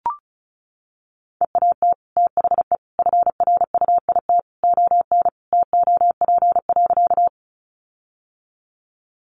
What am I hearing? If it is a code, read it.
Morse code, heard as a keyed tone, about 34 wpm, 700 Hz. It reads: EAT THE FRUIT ON TOP.
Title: Sentences from Top 1300 Words 35wpm